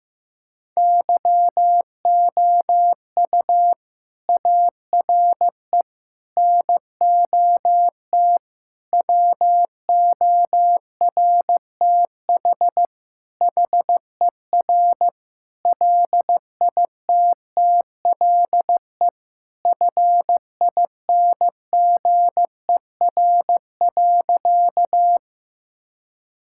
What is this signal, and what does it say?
Morse code, heard as a keyed tone, about 15 wpm, 700 Hz. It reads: YOU ARE NOT WORTH HER LITTLE FINGER.
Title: Sentences from Top 1600 Words 15wpm